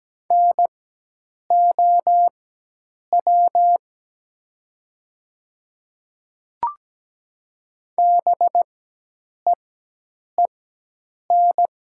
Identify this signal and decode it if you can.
Morse code, heard as a keyed tone, about 17 wpm, spaced as for 8 wpm, 700 Hz. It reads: NOW BEEN
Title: 4x Character Spacing - Top 100 Words - 17wpm